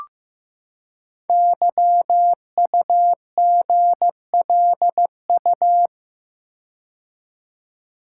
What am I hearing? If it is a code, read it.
Morse code, heard as a keyed tone, about 15 wpm, 700 Hz. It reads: YUGLU
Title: Code Groups - 5 Letter:Numbers - 15wpm